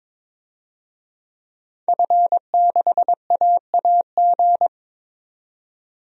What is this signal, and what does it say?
Morse code, heard as a keyed tone, about 22 wpm, 700 Hz. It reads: F6AAG
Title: DXCC-100 Call Signs - 5 Characters - 22wpm